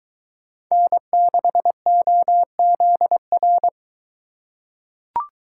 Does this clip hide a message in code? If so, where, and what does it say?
Yes: Morse code from 0.7 to 3.7 s, heard as a keyed tone, about 21 wpm, 700 Hz. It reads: N6OZR